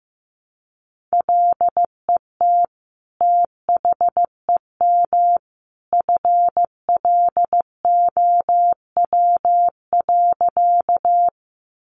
Morse code, heard as a keyed tone, about 15 wpm, 700 Hz. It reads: LET THEM FLOW.
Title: Sentences from Top 1200 Words 15wpm